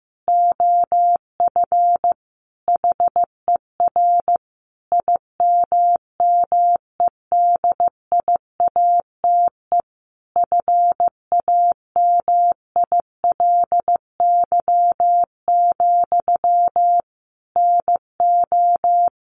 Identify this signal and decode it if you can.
Morse code, heard as a keyed tone, about 15 wpm, 700 Hz. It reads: OF HER IMMEDIATE FAMILY, NO
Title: Sentences from Top 1200 Words 15wpm